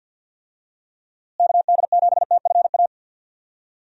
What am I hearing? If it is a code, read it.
Morse code, heard as a keyed tone, about 50 wpm, 700 Hz. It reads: KD7TFA
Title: USA Call Signs - 6 Characters - Repeated 3x - 50wpm